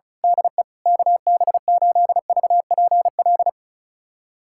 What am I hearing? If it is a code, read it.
Morse code, heard as a keyed tone, about 34 wpm, 690 Hz. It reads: DE KB8VPL
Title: USA Callsign DE Callsign - 6 Characters - 35wpm